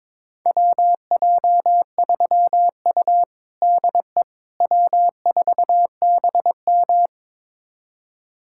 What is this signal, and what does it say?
Morse code, heard as a keyed tone, about 22 wpm, 700 Hz. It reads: WJ3U DE W4BM
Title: USA Callsign DE Callsign - 4 Characters - 22wpm